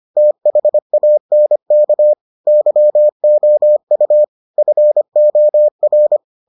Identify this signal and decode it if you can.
Morse code, heard as a keyed tone, about 25 wpm, 600 Hz. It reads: THANK YOU FOR